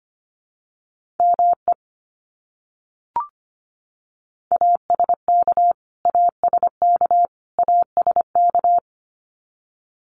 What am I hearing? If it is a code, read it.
Morse code, heard as a keyed tone, about 25 wpm, 700 Hz. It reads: ME ASK ASK ASK
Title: Top 200 Words Repeated 25wpm